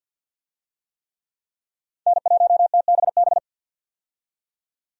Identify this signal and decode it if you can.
Morse code, heard as a keyed tone, about 52 wpm, 700 Hz. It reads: N1TBB
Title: USA Call Signs - 5 Characters - 50wpm